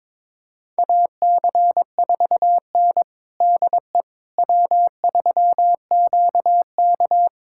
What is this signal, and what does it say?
Morse code, heard as a keyed tone, about 21 wpm, 700 Hz. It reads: AC4N DE W3QK
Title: USA Callsign DE Callsign - 4 Characters - 22wpm